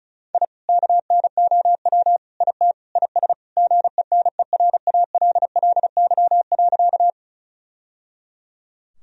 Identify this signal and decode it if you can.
Morse code, heard as a keyed tone, about 36 wpm, 700 Hz. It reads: I KNOW IT IS GENERALLY.
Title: Sentences from Top 1000 Words - Review Part 4 - 35wpm